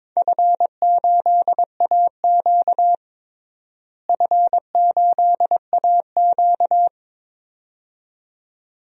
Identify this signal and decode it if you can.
Morse code, heard as a keyed tone, about 22 wpm, 700 Hz. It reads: F8AQ F8AQ
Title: DXCC-100 Call Signs - 4 Characters - Repeated 2x - 22wpm